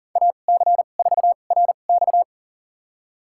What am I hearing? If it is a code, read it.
Morse code, heard as a keyed tone, about 40 wpm, spaced as for 29 wpm, 700 Hz. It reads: AC4RX
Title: USA Call Signs - 5 Characters - 2x Character Spacing - 40wpm